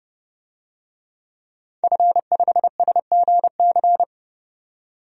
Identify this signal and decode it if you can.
Morse code, heard as a keyed tone, about 30 wpm, 700 Hz. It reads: F5SGC
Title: DXCC-100 Call Signs - 5 Characters - 30wpm